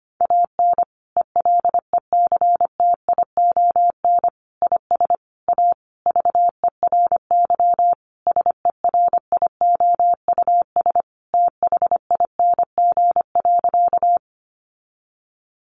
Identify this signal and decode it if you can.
Morse code, heard as a keyed tone, about 25 wpm, 700 Hz. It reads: AN ELECTION IS A VERY SERIOUS THING.
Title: Sentences from Top 1500 Words 25wpm